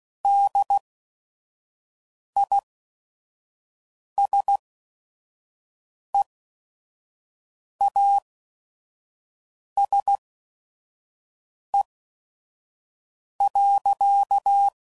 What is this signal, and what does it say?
Morse code, heard as a keyed tone, about 16 wpm, spaced as for 5 wpm, 800 Hz. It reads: DISEASE.
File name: Sayings / Quotes / HoofInMouth